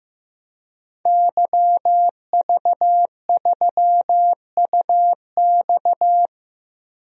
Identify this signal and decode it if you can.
Morse code, heard as a keyed tone, about 15 wpm, 700 Hz. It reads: YV3UX
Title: DXCC-100 Call Signs - 5 Characters - Repeated 3x - 15wpm